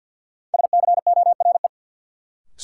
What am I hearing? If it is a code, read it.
Morse code, heard as a keyed tone, about 51 wpm, 700 Hz. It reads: SCORE
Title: Sentences from Top 1600 Words - New Word Drill - 50wpm